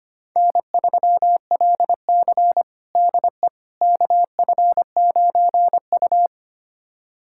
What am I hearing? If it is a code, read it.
Morse code, heard as a keyed tone, about 25 wpm, 700 Hz. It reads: N3LC DE KF9U
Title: USA Callsign DE Callsign - 4 Characters - 25wpm